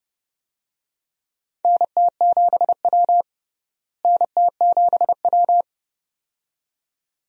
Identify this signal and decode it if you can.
Morse code, heard as a keyed tone, about 30 wpm, 700 Hz. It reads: NT7W NT7W
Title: USA Call Signs - 4 Characters - Repeated 2x - 30wpm